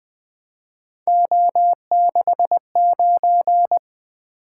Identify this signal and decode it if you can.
Morse code, heard as a keyed tone, about 23 wpm, 700 Hz. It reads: O69
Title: Speed-Racing - Three Letter-Number Combinations - 20wpm